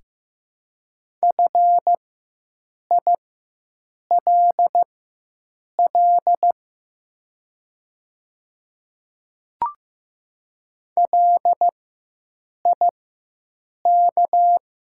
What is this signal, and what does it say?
Morse code, heard as a keyed tone, about 15 wpm, spaced as for 7 wpm, 700 Hz. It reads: FILL LIK
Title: 4x Character Spacing - Top 500 Words - 15wpm